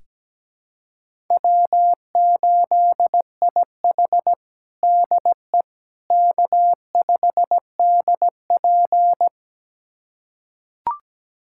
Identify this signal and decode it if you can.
Morse code, heard as a keyed tone, about 17 wpm, 700 Hz. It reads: W8IH DE K5DP
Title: USA Callsign DE Callsign - 4 Characters - 17wpm